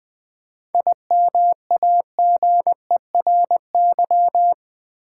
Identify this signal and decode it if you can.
Morse code, heard as a keyed tone, about 20 wpm, 700 Hz. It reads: IMAGERY
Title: 7 Letter Words - First 3 Letters Given - 20wpm